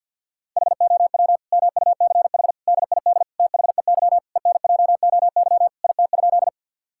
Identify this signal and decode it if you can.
Morse code, heard as a keyed tone, about 53 wpm, 700 Hz. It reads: HOW MUCH DID THEY ENJOY IT?
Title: Sentences from Top 1000 Words - Review Part 5 - 50wpm